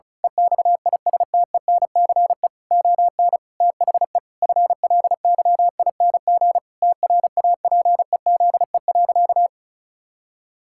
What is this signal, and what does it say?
Morse code, heard as a keyed tone, about 36 wpm, 700 Hz. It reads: EXISTENCE ON THE FLYING TRAPEZE.
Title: Pangram Challenge v2 - 35wpm